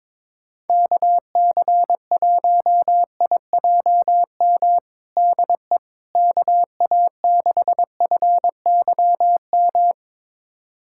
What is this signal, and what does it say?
Morse code, heard as a keyed tone, about 22 wpm, 700 Hz. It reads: KC1IJM DE KA6FYM
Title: USA Callsign DE Callsign - 6 Characters - 22wpm